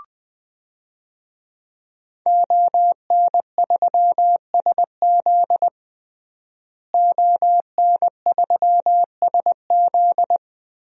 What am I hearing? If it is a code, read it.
Morse code, heard as a keyed tone, about 20 wpm, 700 Hz. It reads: ON3SZ ON3SZ
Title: DXCC-100 Call Signs - 5 Characters - Repeated 3x - 20wpm